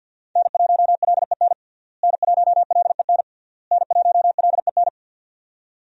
Morse code, heard as a keyed tone, about 52 wpm, 700 Hz. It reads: N1LEN N1LEN N1LEN
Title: USA Call Signs - 5 Characters - Repeated 3x - 50wpm